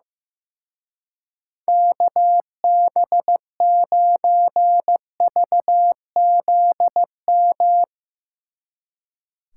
Morse code, heard as a keyed tone, about 16 wpm, 700 Hz. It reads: KB9VZM
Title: Speed Racing - USA 2x3 Call Signs 15wpm